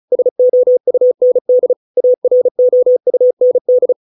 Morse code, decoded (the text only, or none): SOUND AROUND